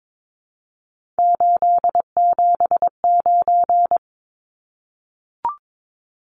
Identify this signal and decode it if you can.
Morse code, heard as a keyed tone, about 22 wpm, 700 Hz. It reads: 879
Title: Numbers - 3-Digit 22wpm